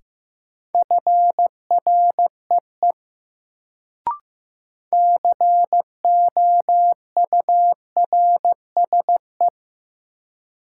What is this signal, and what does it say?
Morse code, heard as a keyed tone, about 15 wpm, 700 Hz. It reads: FREE COURSE
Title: Sentences from Top 500 Words - New Word Drill - 15wpm